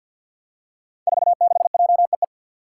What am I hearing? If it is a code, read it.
Morse code, heard as a keyed tone, about 53 wpm, 700 Hz. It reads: 46JEE
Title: Five Challenging Letter-Number Combinations 50wpm